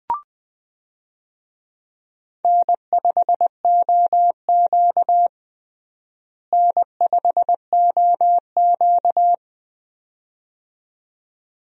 Morse code, heard as a keyed tone, about 21 wpm, 700 Hz. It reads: N5OQ N5OQ
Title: USA Call Signs - 4 Characters - Repeated 2x - 20wpm